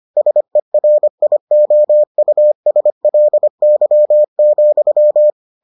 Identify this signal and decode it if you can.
Morse code, heard as a keyed tone, about 25 wpm, 600 Hz. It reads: SERIOUSLY,